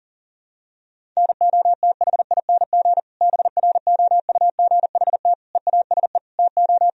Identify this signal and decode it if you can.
Morse code, heard as a keyed tone, about 37 wpm, 700 Hz. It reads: NOTHING BROUGHT EASE TO
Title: Sentences from Top 1000 Words - Review Part 1 - 40wpm